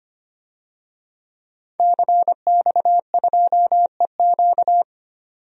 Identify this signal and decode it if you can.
Morse code, heard as a keyed tone, about 25 wpm, 700 Hz. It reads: CX2EQ
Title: DXCC-100 Call Signs - 5 Characters - 25wpm